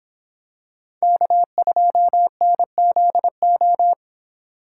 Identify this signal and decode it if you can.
Morse code, heard as a keyed tone, about 26 wpm, 700 Hz. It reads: K2NZO